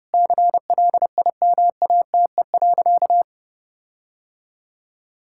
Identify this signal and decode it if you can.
Morse code, heard as a keyed tone, about 32 wpm, 700 Hz. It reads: CLIMATE.
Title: Sentences from Top 1600 Words 30wpm